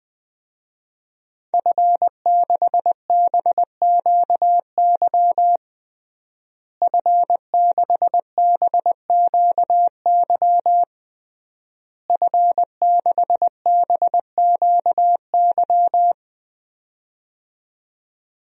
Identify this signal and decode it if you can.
Morse code, heard as a keyed tone, about 20 wpm, 700 Hz. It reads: F6BQY F6BQY F6BQY
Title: DXCC-100 Call Signs - 5 Characters - Repeated 3x - 20wpm